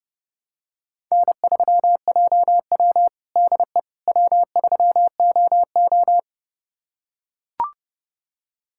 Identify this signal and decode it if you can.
Morse code, heard as a keyed tone, about 30 wpm, 700 Hz. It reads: N3JW DE W3OO
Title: USA Callsign DE Callsign - 4 Characters - 30wpm